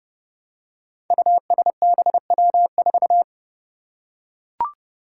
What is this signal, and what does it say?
Morse code, heard as a keyed tone, about 29 wpm, 700 Hz. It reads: USBW4
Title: Five Challenging Letter-Number Combinations 30wpm